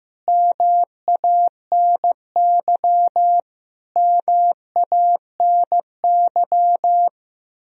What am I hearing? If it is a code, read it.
Morse code, heard as a keyed tone, about 15 wpm, 700 Hz. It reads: MANY MANY
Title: Top 100 Words Repeated 15wpm